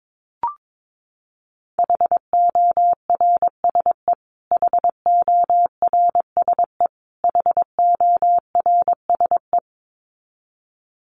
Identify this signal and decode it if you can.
Morse code, heard as a keyed tone, about 22 wpm, 700 Hz. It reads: HORSE HORSE HORSE
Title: Top 500 Words Repeated 22wpm